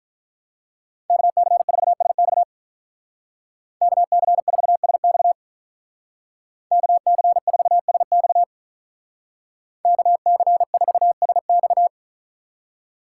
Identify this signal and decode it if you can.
Morse code, heard as a keyed tone, about 47 wpm, 700 Hz. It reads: KC4SX KC4SX KC4SX KC4SX